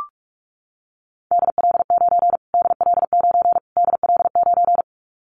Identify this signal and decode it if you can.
Morse code, heard as a keyed tone, about 44 wpm, 700 Hz. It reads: DL9 DL9 DL9
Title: Callsign Prefixes for Top 25 DXCC - Repeated 3x - 45wpm